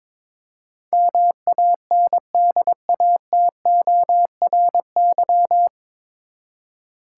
Morse code, heard as a keyed tone, about 21 wpm, 700 Hz. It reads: MANDATORY
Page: Morse Code Ninja practice 22wpm